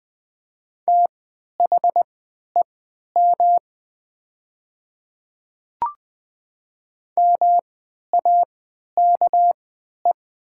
Morse code, heard as a keyed tone, about 20 wpm, spaced as for 11 wpm, 700 Hz. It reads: THEM MAKE
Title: 3x Character Spacing - Top 300 Words - 20wpm